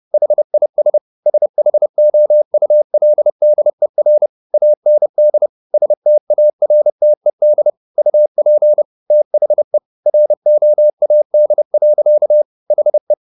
Morse code, heard as a keyed tone, about 30 wpm, 600 Hz. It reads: HIS SHOULDER AND STARTED UP THE ROAD. HE